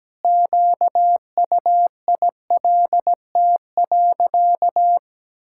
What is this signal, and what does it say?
Morse code, heard as a keyed tone, about 17 wpm, 700 Hz. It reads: QUILT.
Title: Pangram Challenge v2 - 17wpm